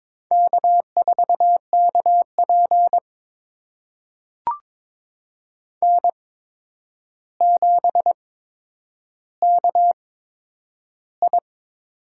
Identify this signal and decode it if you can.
Morse code, heard as a keyed tone, about 21 wpm, spaced as for 9 wpm, 700 Hz. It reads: K4KP N7KI
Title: USA Call Signs - 4 Characters - 8x Character Spacing - 22wpm